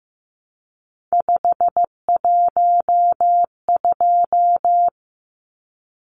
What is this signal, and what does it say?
Morse code, heard as a keyed tone, about 15 wpm, 700 Hz. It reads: 512
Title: Numbers - 3-Digit 15wpm